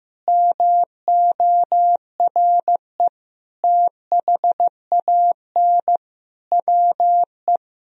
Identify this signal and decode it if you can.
Morse code, heard as a keyed tone, about 16 wpm, 700 Hz. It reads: MORE THAN WE
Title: Sentences from Top 100 Words and Abbreviations 15wpm